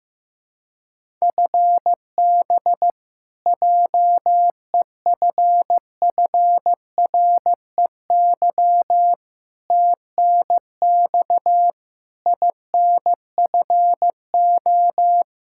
Morse code, heard as a keyed tone, about 15 wpm, 700 Hz. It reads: FB JEFFREY TNX INFO
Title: Practice QSO - Exchange 3 - More Info - 15wpm